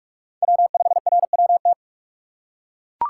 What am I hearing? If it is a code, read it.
Morse code, heard as a keyed tone, about 42 wpm, 700 Hz. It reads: W5RWT
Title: USA Call Signs - 5 Characters - 4x Character Spacing - 45wpm